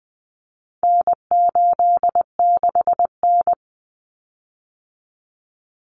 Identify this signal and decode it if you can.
Morse code, heard as a keyed tone, about 20 wpm, 700 Hz. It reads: N86N
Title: Common Cut Numbers - 4-Digit 20wpm